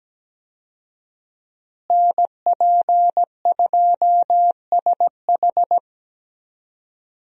Morse code, heard as a keyed tone, about 17 wpm, 700 Hz. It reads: NP2SH
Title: USA Call Signs - 5 Characters - Repeated 2x - 17wpm